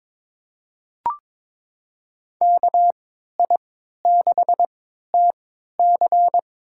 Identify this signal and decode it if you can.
Morse code, heard as a keyed tone, about 22 wpm, spaced as for 13 wpm, 700 Hz. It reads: KI6TC
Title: USA Call Signs - 5 Characters - 3x Character Spacing - 22wpm